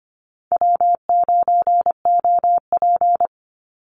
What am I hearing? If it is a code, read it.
Morse code, heard as a keyed tone, about 25 wpm, 700 Hz. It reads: W9OP
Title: Call Signs Repeated 2x - 25wpm